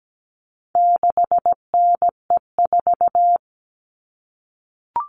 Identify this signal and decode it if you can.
Morse code, heard as a keyed tone, about 17 wpm, 700 Hz. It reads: 6NE4
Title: All Cut Numbers - 4-Digit 17wpm